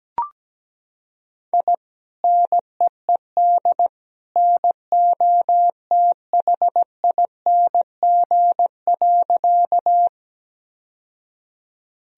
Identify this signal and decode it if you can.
Morse code, heard as a keyed tone, about 17 wpm, 700 Hz. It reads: I NEED NOTHING.
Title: Sentences from Top 1000 Words - Review Part 1 - 17wpm